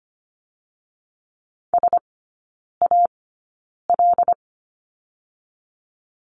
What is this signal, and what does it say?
Morse code, heard as a keyed tone, about 25 wpm, spaced as for 9 wpm, 700 Hz. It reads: SAL
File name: Lesson 15e - Number 5 - Three Characters 25wpm